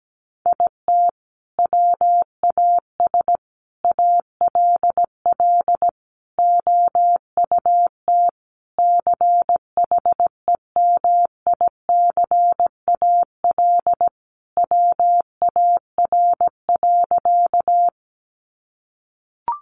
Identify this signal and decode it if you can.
Morse code, heard as a keyed tone, about 17 wpm, 700 Hz. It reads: IT WAS ALL OUT CHEMICAL WAR.
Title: Sentences from Top 1200 Words 17wpm